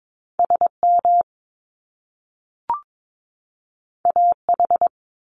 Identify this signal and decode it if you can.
Morse code, heard as a keyed tone, about 22 wpm, 700 Hz. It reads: SM AH
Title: Top Two Letter Combinations in Words 22wpm